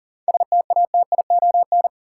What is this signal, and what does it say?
Morse code, heard as a keyed tone, about 40 wpm, 700 Hz. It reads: STATION
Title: Seven Letter Words - 40wpm